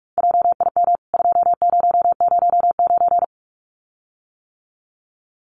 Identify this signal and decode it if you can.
Morse code, heard as a keyed tone, about 44 wpm, 700 Hz. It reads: JIM 2009
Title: CWT Exchange 45wpm